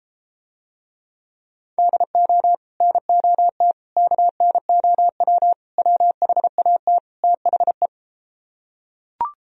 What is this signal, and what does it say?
Morse code, heard as a keyed tone, about 33 wpm, 700 Hz. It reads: DO NOT KNOW WHAT THE